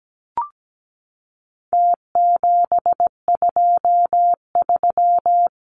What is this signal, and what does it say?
Morse code, heard as a keyed tone, about 17 wpm, 700 Hz. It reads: T723
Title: Numbers - Contest Style - 4-Digit 17wpm